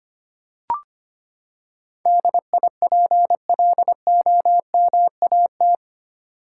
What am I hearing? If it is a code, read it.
Morse code, heard as a keyed tone, about 25 wpm, 700 Hz. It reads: DIPLOMAT
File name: Eight Letter Words - 25wpm